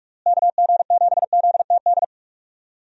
Morse code, heard as a keyed tone, about 46 wpm, 700 Hz. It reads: KG7ZTD
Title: USA Call Signs - 6 Characters - Repeated 3x - 45wpm